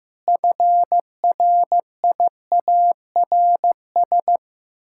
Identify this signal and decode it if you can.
Morse code, heard as a keyed tone, about 15 wpm, 700 Hz. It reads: FRIARS
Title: Pangram Challenge v2 - 15wpm